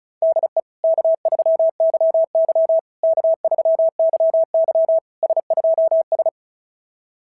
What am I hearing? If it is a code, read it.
Morse code, heard as a keyed tone, about 36 wpm, 690 Hz. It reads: DE K3YY K3YY S2S
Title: SOTA QSO Practice 35wpm